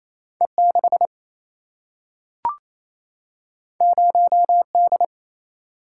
Morse code, heard as a keyed tone, about 28 wpm, 700 Hz. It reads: E 6 0 D
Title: Single-Letter Single-Number 28wpm